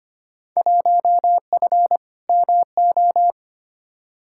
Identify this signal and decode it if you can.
Morse code, heard as a keyed tone, about 25 wpm, 700 Hz. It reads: 1F MO
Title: Field Day Exchanges Encore - 25wpm